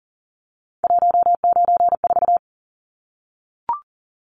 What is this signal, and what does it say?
Morse code, heard as a keyed tone, about 40 wpm, 700 Hz. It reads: 194
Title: Numbers - 3-Digit 40wpm